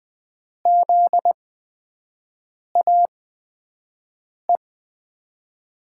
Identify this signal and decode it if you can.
Morse code, heard as a keyed tone, about 20 wpm, spaced as for 5 wpm, 700 Hz. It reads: ZAE
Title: USA Call Signs - 8x Character Spacing - 20wpm